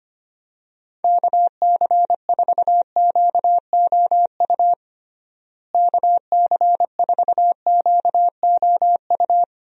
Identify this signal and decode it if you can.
Morse code, heard as a keyed tone, about 25 wpm, 700 Hz. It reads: KC4QOU KC4QOU
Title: USA Call Signs - 6 Characters - Repeated 2x - 25wpm